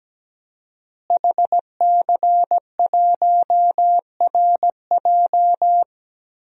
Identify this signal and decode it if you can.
Morse code, heard as a keyed tone, about 17 wpm, 700 Hz. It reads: HC1RJ
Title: DXCC-100 Call Signs - 5 Characters - Repeated 3x - 17wpm